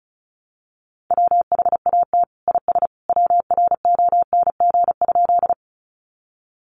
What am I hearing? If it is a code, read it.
Morse code, heard as a keyed tone, about 34 wpm, 700 Hz. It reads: WHAT IS WRONG?
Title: Sentences from Top 900 Words 35wpm